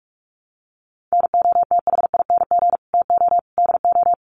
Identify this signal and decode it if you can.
Morse code, heard as a keyed tone, about 44 wpm, 700 Hz. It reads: NOTHING TO DO
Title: Sets of 5 Words Encore 45wpm